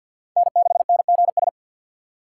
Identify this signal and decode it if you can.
Morse code, heard as a keyed tone, about 51 wpm, 700 Hz. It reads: N6NGS
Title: USA Call Signs - 6x Character Spacing - 50wpm